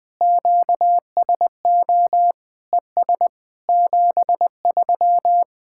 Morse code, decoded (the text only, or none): QSO ES 73